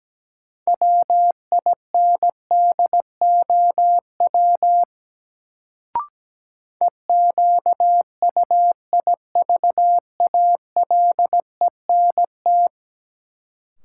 Morse code, heard as a keyed tone, about 17 wpm, 700 Hz. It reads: WINDOW EQUIVALENT
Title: Sentences from Top 1600 Words - New Word Drill Alt - 17wpm